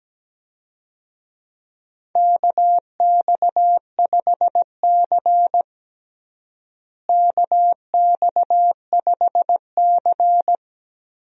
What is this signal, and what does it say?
Morse code, heard as a keyed tone, about 17 wpm, 700 Hz. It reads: KX5C KX5C
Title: USA Call Signs - 4 Characters - Repeated 2x - 17wpm